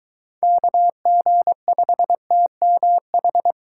Morse code, heard as a keyed tone, about 22 wpm, 700 Hz. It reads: KG5TMH